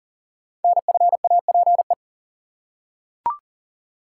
Morse code, heard as a keyed tone, about 40 wpm, 700 Hz. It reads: NFAPE
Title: Code Groups - 5 Letters - 40wpm